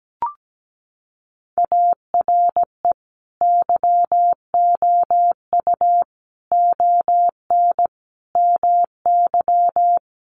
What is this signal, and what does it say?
Morse code, heard as a keyed tone, about 17 wpm, 700 Hz. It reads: ARE YOU ON MY
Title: Sentences from Top 1500 Words 17wpm